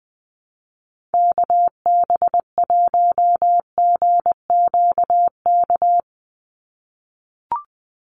Morse code, heard as a keyed tone, about 20 wpm, 700 Hz. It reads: KB1GQK
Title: USA Call Signs 20wpm